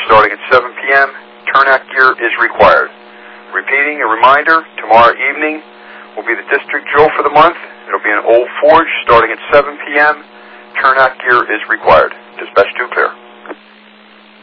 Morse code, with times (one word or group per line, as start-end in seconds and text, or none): none